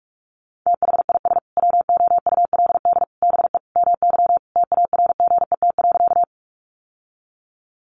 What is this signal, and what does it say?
Morse code, heard as a keyed tone, about 44 wpm, 700 Hz. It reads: THIS WOULD BE MY TARGET.
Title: Sentences from Top 1200 Words 45wpm